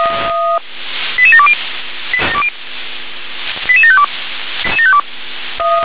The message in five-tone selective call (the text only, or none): 88720 88752